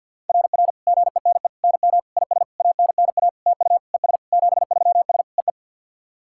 Morse code, heard as a keyed tone, about 50 wpm, 690 Hz. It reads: RR GENE NM IS ANNA TU ES 73S EE
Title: SOTA QSO Practice 50wpm